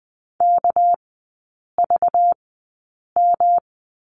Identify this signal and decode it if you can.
Morse code, heard as a keyed tone, about 20 wpm, spaced as for 8 wpm, 690 Hz. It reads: KVM
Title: 198 Lesson 32e - Letter k - Three Characters 20wpm